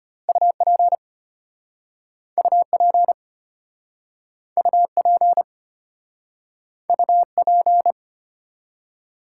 Morse code, heard as a keyed tone, about 33 wpm, 700 Hz. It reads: UP UP UP UP